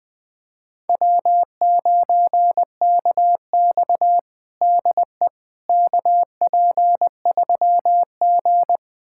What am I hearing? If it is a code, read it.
Morse code, heard as a keyed tone, about 21 wpm, 700 Hz. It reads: W9KX DE KP3G
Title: USA Callsign DE Callsign - 4 Characters - 20wpm